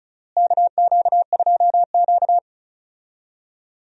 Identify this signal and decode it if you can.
Morse code, heard as a keyed tone, about 36 wpm, 700 Hz. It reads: KQ2Q
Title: USA Call Signs - 4 Characters - Repeated 2x - 35wpm